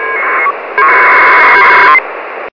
Packet radio (AX.25, as in APRS).